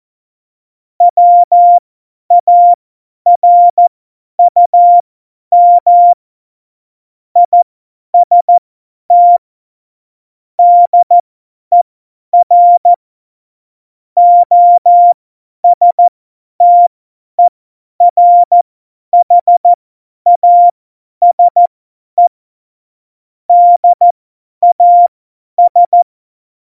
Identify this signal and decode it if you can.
Morse code, heard as a keyed tone, about 14 wpm, spaced as for 10 wpm, 700 Hz. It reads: WARUM IST DER OSTERHASE DAS
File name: Morsekurs / Witze / Witz 28 April 2020, 50 F70